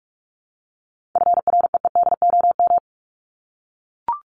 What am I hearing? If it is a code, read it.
Morse code, heard as a keyed tone, about 44 wpm, 700 Hz. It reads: FREEDOM
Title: Top 501-750 Words 45wpm